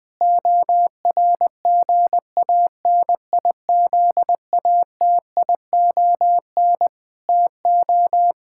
Morse code, heard as a keyed tone, about 20 wpm, 700 Hz. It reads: ORGANIZATION TO
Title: Sentences from Top 1000 Words - Review Part 5 - 20wpm